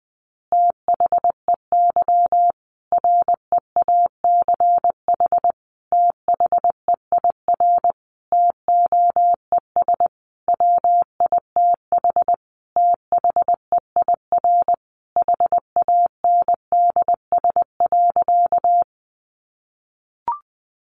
Morse code, heard as a keyed tone, about 20 wpm, 700 Hz. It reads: THEY REACH THEIR TOES WITH THEIR HANDS.
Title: Easy Phrases 20wpm